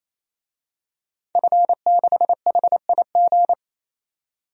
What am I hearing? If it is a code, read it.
Morse code, heard as a keyed tone, about 28 wpm, 700 Hz. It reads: F6HIG
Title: DXCC-100 Call Signs - 5 Characters - 28wpm